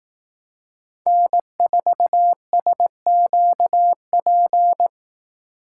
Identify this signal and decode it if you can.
Morse code, heard as a keyed tone, about 18 wpm, 700 Hz. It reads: N4SQP
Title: Speed Racing - USA 1x3 Call Signs 15wpm